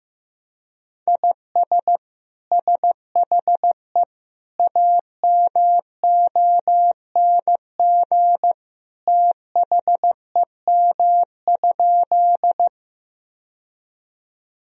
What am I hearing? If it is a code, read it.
Morse code, heard as a keyed tone, about 15 wpm, 700 Hz. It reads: IS SHE AMONG THEM?
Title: Sentences from Top 1000 Words - Review Part 3 - 15wpm